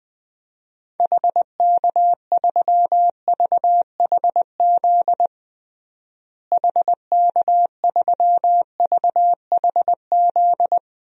Morse code, heard as a keyed tone, about 21 wpm, 700 Hz. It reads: HK3VHZ HK3VHZ
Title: DXCC-100 Call Signs - 6 Characters - Repeated 2x - 20wpm